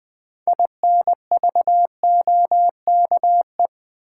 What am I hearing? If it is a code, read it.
Morse code, heard as a keyed tone, about 21 wpm, 700 Hz. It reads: INVOKE
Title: Build A Word - 2 Letters At A Time - 6-Letter Words - 20wpm